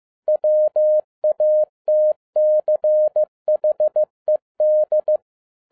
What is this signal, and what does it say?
Morse code, heard as a keyed tone, about 15 wpm, 610 Hz. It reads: WATCHED